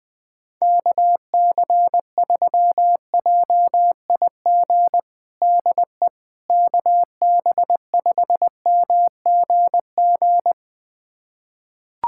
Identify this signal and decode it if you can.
Morse code, heard as a keyed tone, about 20 wpm, 700 Hz. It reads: KC3JIG DE KB5MGG
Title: USA Callsign DE Callsign 20wpm